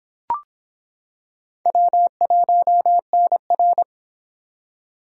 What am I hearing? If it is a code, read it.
Morse code, heard as a keyed tone, about 26 wpm, 700 Hz. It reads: W1NR